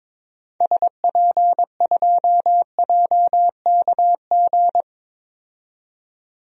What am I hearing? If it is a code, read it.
Morse code, heard as a keyed tone, about 22 wpm, 700 Hz. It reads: SP2JKG
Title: DXCC-100 Call Signs - 6 Characters - 22wpm